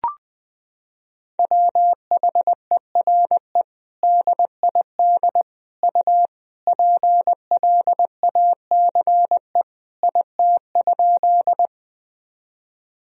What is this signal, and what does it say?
Morse code, heard as a keyed tone, about 21 wpm, 700 Hz. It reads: WHERE DID U PLACE IT?
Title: Sentences from Top 200 Words and Abbreviations 20wpm